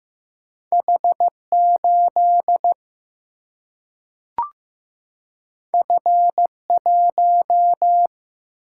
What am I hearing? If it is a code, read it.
Morse code, heard as a keyed tone, about 15 wpm, 700 Hz. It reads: H8 F1
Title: Single-Letter Single-Number 15wpm